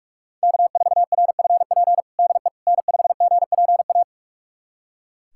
Morse code, heard as a keyed tone, about 41 wpm, 700 Hz. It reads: K4RFP DE N5GPA
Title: USA Callsign DE Callsign - 5 Characters - 45wpm